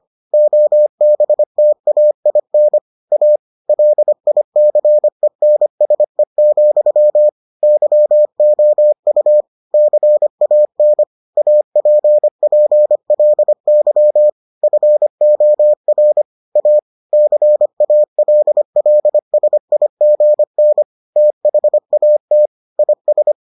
Morse code, heard as a keyed tone, about 25 wpm, 600 Hz. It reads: OBTAIN A LICENSE, YOU CAN APPLY FOR A CALLSIGN THAT IS